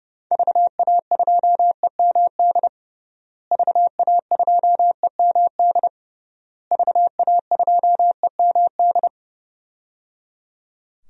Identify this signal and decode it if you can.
Morse code, heard as a keyed tone, about 30 wpm, 700 Hz. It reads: VA2EMD VA2EMD VA2EMD
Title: DXCC-100 Call Signs - 6 Characters - Repeated 3x - 30wpm